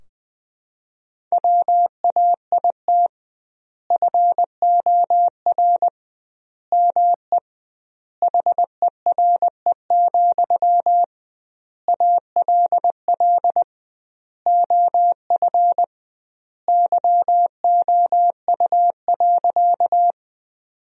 Morse code, heard as a keyed tone, about 20 wpm, 700 Hz. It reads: WAIT FOR ME HERE, ALL OF YOU.
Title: 2x Word-Spacing - Sentences from Top 500 Words - 20wpm